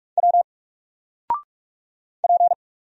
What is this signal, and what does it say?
Morse code, heard as a keyed tone, about 44 wpm, 700 Hz. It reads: W P
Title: Single Character (Letter:Number:Punctuation:Prosign) 45wpm